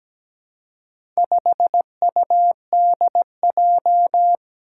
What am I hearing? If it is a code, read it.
Morse code, heard as a keyed tone, about 17 wpm, 700 Hz. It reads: 5UDJ
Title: Four Challenging Letter-Number Combinations 17wpm